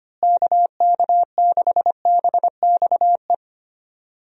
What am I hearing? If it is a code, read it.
Morse code, heard as a keyed tone, about 25 wpm, 700 Hz. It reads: KK6BXE
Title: USA Call Signs - 6 Characters - Repeated 3x - 25wpm